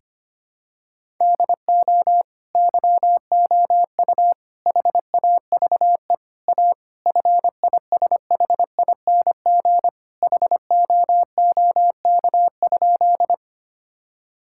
Morse code, heard as a keyed tone, about 25 wpm, 700 Hz. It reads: DO YOU HAVE A FISHING HOOK?
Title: Sentences from Top 1600 Words 25wpm